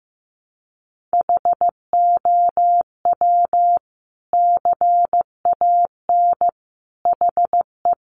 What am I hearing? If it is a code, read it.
Morse code, heard as a keyed tone, about 15 wpm, 700 Hz. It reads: HOW CAN HE
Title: Sentences from Top 1100 Words 15wpm